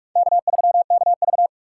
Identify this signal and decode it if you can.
Morse code, heard as a keyed tone, about 44 wpm, 690 Hz. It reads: K3KV
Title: USA Call Signs - 4 Characters - 4x Character Spacing - 45wpm